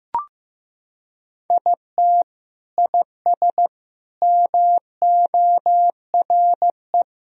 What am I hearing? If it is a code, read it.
Morse code, heard as a keyed tone, about 15 wpm, 700 Hz. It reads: IT IS MORE
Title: Sentences from Top 1000 Words - Review Part 1 - 15wpm